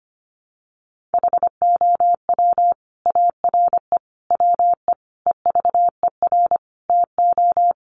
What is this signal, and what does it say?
Morse code, heard as a keyed tone, about 25 wpm, 700 Hz. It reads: HOW ARE WE EVER TO
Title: Sentences from Top 1500 Words 25wpm